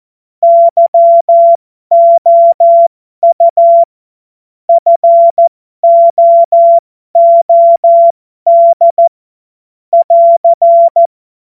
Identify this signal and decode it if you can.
Morse code, heard as a keyed tone, about 14 wpm, spaced as for 12 wpm, 680 Hz. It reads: YOU FOOD <AR>
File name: Morsekurs / Witze